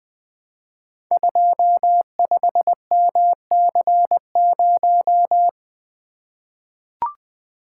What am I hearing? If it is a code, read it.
Morse code, heard as a keyed tone, about 21 wpm, 700 Hz. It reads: 25MC0
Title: Code Groups - 5 Letter:Numbers - 20wpm